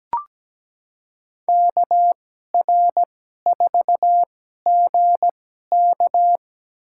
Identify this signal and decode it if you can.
Morse code, heard as a keyed tone, about 17 wpm, spaced as for 12 wpm, 700 Hz. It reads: KR4GK
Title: USA Call Signs - 2x Character Spacing - 17wpm